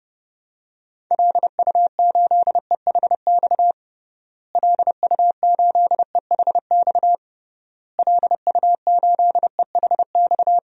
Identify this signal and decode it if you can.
Morse code, heard as a keyed tone, about 32 wpm, 700 Hz. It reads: LU8EHX LU8EHX LU8EHX
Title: DXCC-100 Call Signs - 6 Characters - Repeated 3x - 30wpm